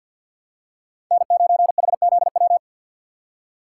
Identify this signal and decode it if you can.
Morse code, heard as a keyed tone, about 53 wpm, 700 Hz. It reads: N9HZW
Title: USA Call Signs - 5 Characters - 2x Character Spacing - 50wpm